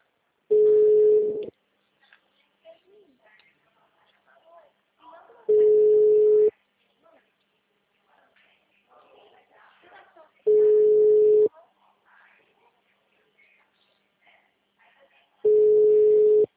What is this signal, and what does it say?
Ringback tone (Europe, 425 Hz), with speech.